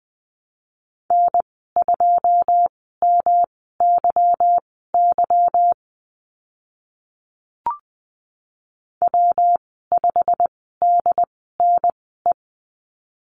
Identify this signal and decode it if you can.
Morse code, heard as a keyed tone, about 20 wpm, spaced as for 14 wpm, 700 Hz. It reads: N2MYY W5DNE
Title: 154 Lesson 25g - Letter p - Call Signs 20wpm